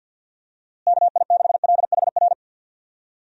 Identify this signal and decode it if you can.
Morse code, heard as a keyed tone, about 46 wpm, 700 Hz. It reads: KI6LHR
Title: USA Call Signs - 6 Characters - Repeated 3x - 50wpm